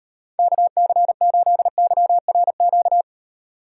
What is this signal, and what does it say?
Morse code, heard as a keyed tone, about 38 wpm, 700 Hz. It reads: KC8YRQ